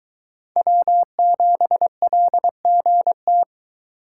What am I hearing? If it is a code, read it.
Morse code, heard as a keyed tone, about 21 wpm, 700 Hz. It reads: W7LGT